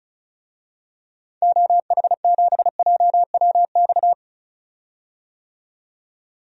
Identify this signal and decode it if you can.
Morse code, heard as a keyed tone, about 36 wpm, 700 Hz. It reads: OH7JWX
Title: DXCC-100 Call Signs - 6 Characters - 35wpm